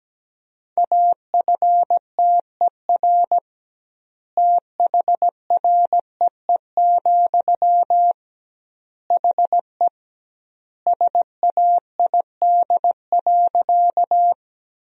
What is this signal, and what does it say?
Morse code, heard as a keyed tone, about 17 wpm, 700 Hz. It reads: AFTER THREE, HE SAID.
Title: 2x Word-Spacing - Sentences from Top 200 Words - 17wpm